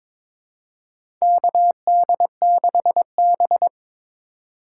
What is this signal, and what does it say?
Morse code, heard as a keyed tone, about 22 wpm, 700 Hz. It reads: KD6B
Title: USA Call Signs - 4 Characters - 6x Character Spacing - 22wpm